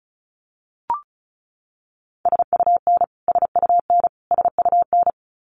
Morse code, heard as a keyed tone, about 34 wpm, 700 Hz. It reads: SUN SUN SUN
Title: Top 500 Words Repeated 35wpm